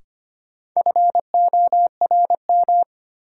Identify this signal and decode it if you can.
Morse code, heard as a keyed tone, about 25 wpm, 700 Hz. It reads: FORM
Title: Sentences from Top 1000 Words - Review Part 1 - New Word Drill - 25wpm